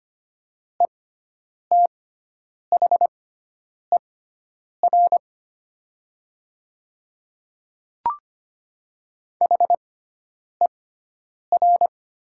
Morse code, heard as a keyed tone, about 25 wpm, spaced as for 9 wpm, 700 Hz. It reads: ETHER HER
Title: 6x Character Spacing - Top 300 Words - 25wpm